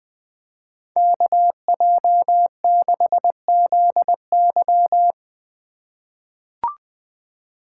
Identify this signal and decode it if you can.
Morse code, heard as a keyed tone, about 20 wpm, 700 Hz. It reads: KJ6ZY
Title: USA Call Signs - 5 Characters - 2x Character Spacing - 20wpm